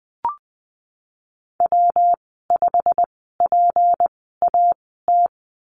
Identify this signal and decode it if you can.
Morse code, heard as a keyed tone, about 20 wpm, spaced as for 15 wpm, 700 Hz. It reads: W5PAT
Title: 186 Lesson 30g - Letter b - Call Signs 20wpm